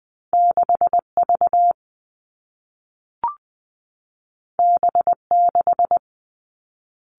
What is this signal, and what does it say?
Morse code, heard as a keyed tone, about 20 wpm, 700 Hz. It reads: 6 V B 6
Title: Two Challenging Letter-Number Combinations 20wpm